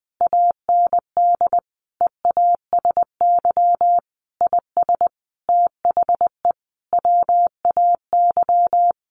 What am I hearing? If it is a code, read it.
Morse code, heard as a keyed tone, about 20 wpm, 700 Hz. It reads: AND EASY IS THE WAY
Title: Sentences from Top 1400 Words 20wpm